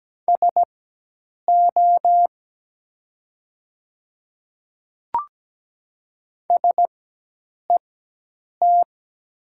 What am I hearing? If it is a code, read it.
Morse code, heard as a keyed tone, about 17 wpm, spaced as for 8 wpm, 700 Hz. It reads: SO SET
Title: 4x Character Spacing - Top 200 Words - 17wpm